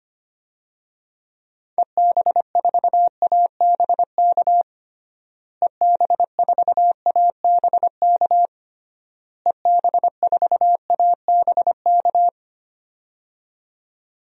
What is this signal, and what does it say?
Morse code, heard as a keyed tone, about 25 wpm, 700 Hz. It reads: EB4ABK EB4ABK EB4ABK
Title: DXCC-100 Call Signs - 6 Characters - Repeated 3x - 25wpm